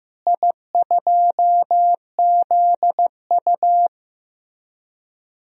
Morse code, heard as a keyed tone, about 16 wpm, 700 Hz. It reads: I2ZU
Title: DXCC-100 Call Signs - 4 Characters - Repeated 3x - 15wpm